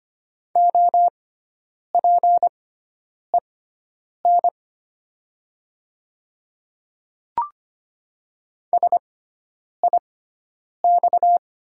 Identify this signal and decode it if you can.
Morse code, heard as a keyed tone, about 25 wpm, spaced as for 9 wpm, 700 Hz. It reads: OPEN SIX
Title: 6x Character Spacing - Top 500 Words - 25wpm